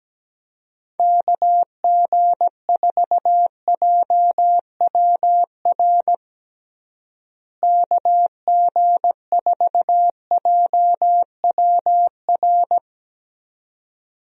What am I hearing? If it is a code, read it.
Morse code, heard as a keyed tone, about 17 wpm, 700 Hz. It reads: KG4JWR KG4JWR
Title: USA Call Signs - 6 Characters - Repeated 2x - 17wpm